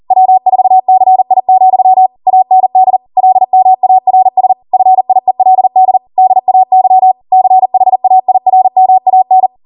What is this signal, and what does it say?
Morse code, heard as a keyed tone, about 40 wpm, 750 Hz. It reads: W4CI, AND LMARS FIELD DAY CHAIRMAN